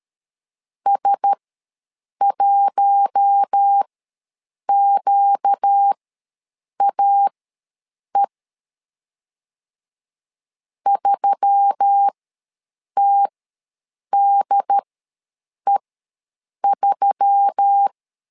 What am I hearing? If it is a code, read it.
Morse code, heard as a keyed tone, about 13 wpm, spaced as for 7 wpm, 800 Hz. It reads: S1QAE 3TDE3
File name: LEZIONI / 4 Q1S3 ws 12wpm 2038